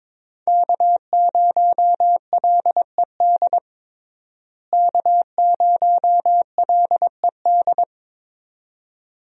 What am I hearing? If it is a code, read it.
Morse code, heard as a keyed tone, about 22 wpm, 700 Hz. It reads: K0LED K0LED
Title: USA Call Signs - 5 Characters - Repeated 2x - 22wpm